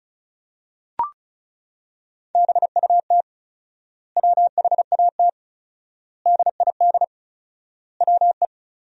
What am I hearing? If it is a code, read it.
Morse code, heard as a keyed tone, about 36 wpm, 700 Hz. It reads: BUT WHAT DID WE
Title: 4x Word-Spacing - Sentences from Top 400 Words - 35wpm